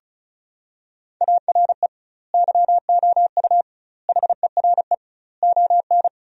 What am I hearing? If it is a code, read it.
Morse code, heard as a keyed tone, about 36 wpm, 700 Hz. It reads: ARE YOU HERE ON
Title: 2x Word-Spacing - Sentences from Top 400 Words - 35wpm